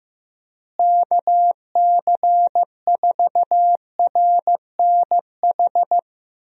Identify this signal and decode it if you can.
Morse code, heard as a keyed tone, about 15 wpm, 700 Hz. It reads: KC4RNH
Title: USA Call Signs - 6 Characters - Repeated 3x - 15wpm